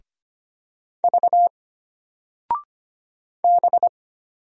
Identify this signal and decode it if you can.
Morse code, heard as a keyed tone, about 25 wpm, 700 Hz. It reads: V B
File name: Single-Letter-Number-25wpm